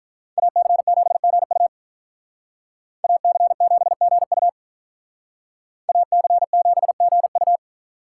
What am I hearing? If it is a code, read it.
Morse code, heard as a keyed tone, about 46 wpm, 700 Hz. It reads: AC7GU AC7GU AC7GU